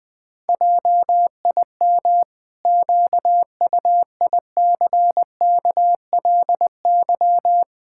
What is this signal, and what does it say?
Morse code, heard as a keyed tone, about 20 wpm, 700 Hz. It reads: JIM QUICKLY
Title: Pangram Challenge v2 - 20wpm